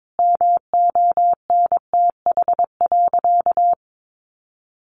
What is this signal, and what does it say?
Morse code, heard as a keyed tone, about 22 wpm, 700 Hz. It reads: MONTH.
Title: Sentences from Top 1200 Words 22wpm